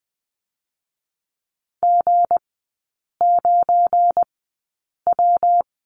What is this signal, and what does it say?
Morse code, heard as a keyed tone, about 20 wpm, 700 Hz. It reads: G 9 W
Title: 172 Lesson 28e - Number 9 - Three Characters 20wpm